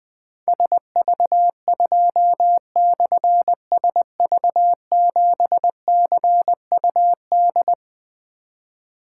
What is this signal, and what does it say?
Morse code, heard as a keyed tone, about 20 wpm, 700 Hz. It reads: SV2/SV7CUD
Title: Difficult Call Signs 20wpm